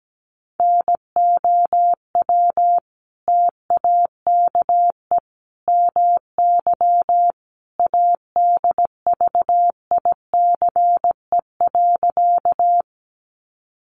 Morse code, heard as a keyed tone, about 17 wpm, 700 Hz. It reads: NOW TAKE MY ADVICE.
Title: Sentences from Top 1100 Words 17wpm